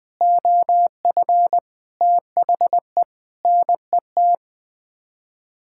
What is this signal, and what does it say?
Morse code, heard as a keyed tone, about 20 wpm, 700 Hz. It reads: OF THE NET
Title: Sets of 8 Words - 20wpm